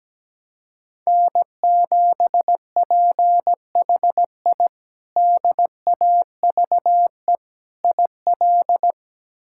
Morse code, heard as a keyed tone, about 17 wpm, 700 Hz. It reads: N7PHI DAVE IL
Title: K1USN SST QSO Practice - 17wpm